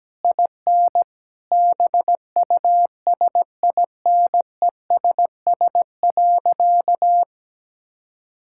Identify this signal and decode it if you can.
Morse code, heard as a keyed tone, about 17 wpm, 700 Hz. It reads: IN BUSINESS.
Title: Sentences from Top 1000 Words - Review Part 6 - 17wpm